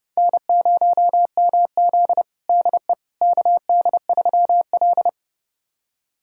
Morse code, heard as a keyed tone, about 30 wpm, 700 Hz. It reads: N0MZ DE KD3L
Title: USA Callsign DE Callsign - 4 Characters - 30wpm